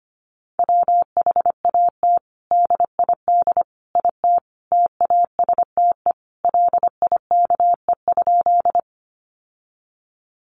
Morse code, heard as a keyed tone, about 25 wpm, 700 Hz. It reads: WHAT DID IT TASTE LIKE?
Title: Sentences from Top 1100 Words 25wpm